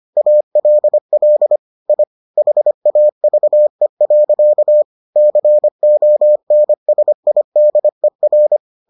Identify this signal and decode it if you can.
Morse code, heard as a keyed tone, about 25 wpm, 600 Hz. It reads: ALL I HAVE. CONSIDER